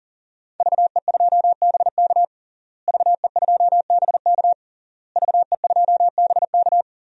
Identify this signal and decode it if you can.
Morse code, heard as a keyed tone, about 40 wpm, 700 Hz. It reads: VE2BK VE2BK VE2BK
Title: DXCC-100 Call Signs - 5 Characters - Repeated 3x - 40wpm